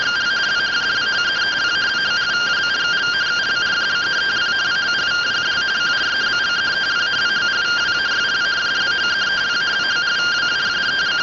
Two-tone teleprinter signal, about 170 Hz shift.
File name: findings / axial iodine 3mm 205.gen-03 rtty rx